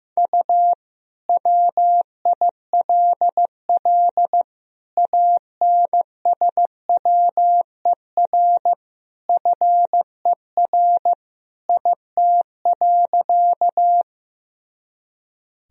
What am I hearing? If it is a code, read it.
Morse code, heard as a keyed tone, about 15 wpm, 700 Hz. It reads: U WILL ANSWER FER IT.
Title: Sentences from Top 300 Words and Abbreviations 15wpm